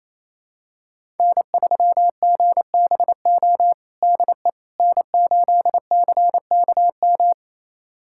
Morse code, heard as a keyed tone, about 28 wpm, 700 Hz. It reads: N3GBO DE N8CKM
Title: USA Callsign DE Callsign - 5 Characters - 28wpm